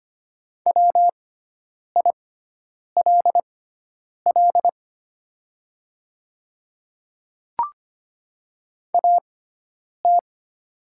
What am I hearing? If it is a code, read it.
Morse code, heard as a keyed tone, about 25 wpm, spaced as for 9 wpm, 700 Hz. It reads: WILL AT
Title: 6x Character Spacing - Top 100 Words - 25wpm